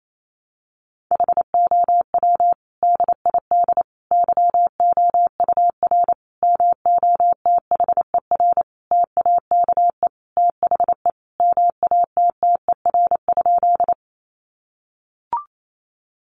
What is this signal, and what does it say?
Morse code, heard as a keyed tone, about 28 wpm, 700 Hz. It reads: HOW DID YOUR MOTHER TAKE THE MATTER?
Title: Sentences from Top 700 Words 28wpm